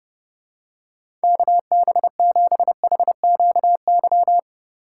Morse code, heard as a keyed tone, about 30 wpm, 700 Hz. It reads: KB7HQY
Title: USA Call Signs - 6x Character Spacing - 30wpm